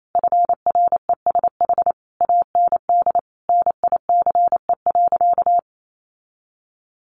Morse code, heard as a keyed tone, about 28 wpm, 700 Hz. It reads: FRESH AND NICE.
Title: Sentences from Top 1200 Words 28wpm